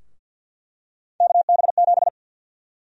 Morse code, heard as a keyed tone, about 50 wpm, 700 Hz. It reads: KB7
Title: USA Call Sign Prefixes - Repeated 2x - 50wpm